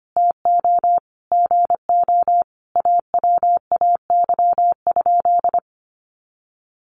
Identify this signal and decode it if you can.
Morse code, heard as a keyed tone, about 25 wpm, 700 Hz. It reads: TO GO AWAY?
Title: Sentences from Top 600 Words 25wpm